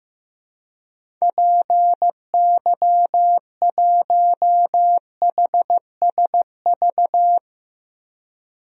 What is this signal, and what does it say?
Morse code, heard as a keyed tone, about 15 wpm, 700 Hz. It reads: PY1HSV
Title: DXCC-100 Call Signs - 6 Characters - Repeated 2x - 15wpm